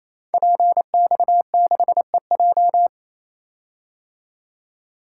Morse code, heard as a keyed tone, about 29 wpm, 700 Hz. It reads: PX6EJ
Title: Code Groups - 5 Letter:Numbers - 28wpm